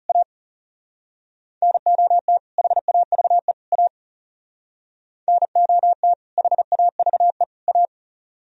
Morse code, heard as a keyed tone, about 35 wpm, 700 Hz. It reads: A NOT HAVE A NOT HAVE A
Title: Speed-Racing - Sets of 3 Words 30wpm